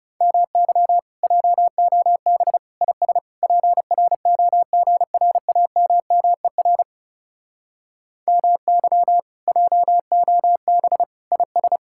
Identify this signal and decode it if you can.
Morse code, heard as a keyed tone, about 40 wpm, 700 Hz. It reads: MY JOB IS PROGRAMMER MY JOB IS